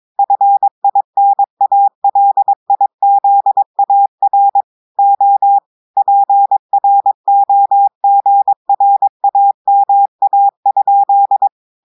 Morse code, heard as a keyed tone, about 23 wpm, 800 Hz. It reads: FINALIZAR O PROGRAMA?
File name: audios / portugais / portugais texte en clair 2 vitesse 22 mots minutes